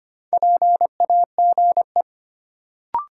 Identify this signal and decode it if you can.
Morse code, heard as a keyed tone, about 25 wpm, 700 Hz. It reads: PAGE
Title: Sentences from Top 1000 Words - Review Part 1 - New Word Drill - 25wpm